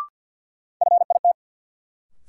Morse code, heard as a keyed tone, about 48 wpm, 700 Hz. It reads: FIT
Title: Sentences from Top 800 Words - New Word Drill - 50wpm